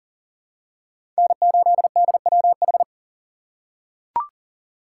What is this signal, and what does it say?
Morse code, heard as a keyed tone, about 40 wpm, 700 Hz. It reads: N8DWH
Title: USA Call Signs - 5 Characters - 3x Character Spacing - 40wpm